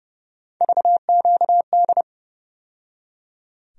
Morse code, heard as a keyed tone, about 31 wpm, 700 Hz. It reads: VQD
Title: USA Call Sign Postfixes - 3 Letters 30wpm